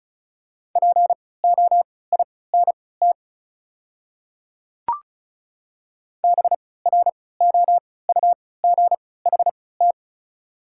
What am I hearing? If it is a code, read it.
Morse code, heard as a keyed tone, about 36 wpm, spaced as for 20 wpm, 700 Hz. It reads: POINT BROUGHT
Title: 3x Character Spacing - Top 500 Words - 35wpm